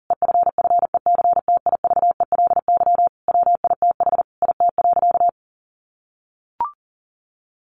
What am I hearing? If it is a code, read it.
Morse code, heard as a keyed tone, about 40 wpm, 700 Hz. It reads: EFFECTIVELY WITH IT.
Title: Sentences from Top 1400 Words 40wpm